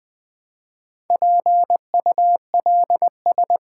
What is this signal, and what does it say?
Morse code, heard as a keyed tone, about 20 wpm, 700 Hz. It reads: PULS
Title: Build A Word - 1 Letter At A Time - 5-Letter Words - 20wpm